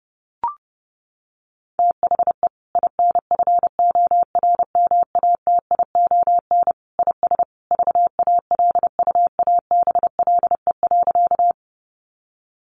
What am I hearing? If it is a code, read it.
Morse code, heard as a keyed tone, about 30 wpm, 700 Hz. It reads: THE INFORMATION IS VALUABLE.
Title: Sentences from Top 1100 Words 30wpm